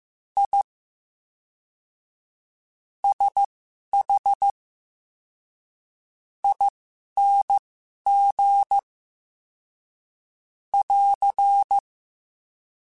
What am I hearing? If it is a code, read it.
Morse code, heard as a keyed tone, about 15 wpm, spaced as for 6 wpm, 800 Hz. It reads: ISHING <AR>
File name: Special INGs / Special INGs-mp3s / VerySlowToFaster Ing @10